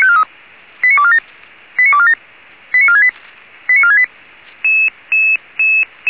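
Five-tone selective call (ZVEI), heard as quick runs of five tones, tones of about 70 ms. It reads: 89247 89247 89468 89468
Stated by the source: ZVEI five-tone selective call, received on SDR.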